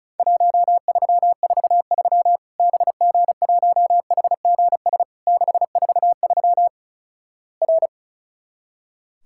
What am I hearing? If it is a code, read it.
Morse code, heard as a keyed tone, about 36 wpm, 700 Hz. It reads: 1343 BG1HGS 643 R
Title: Worked All Europe DX Contest - QTC Exchange - 35wpm